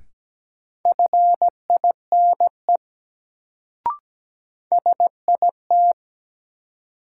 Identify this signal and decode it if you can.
Morse code, heard as a keyed tone, about 17 wpm, 700 Hz. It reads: FINE SIT
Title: Sentences from Top 500 Words - New Word Drill - 17wpm